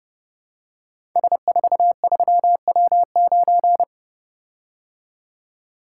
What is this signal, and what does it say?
Morse code, heard as a keyed tone, about 28 wpm, 700 Hz. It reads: S43W9
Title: Code Groups - 5 Letter:Numbers - 30wpm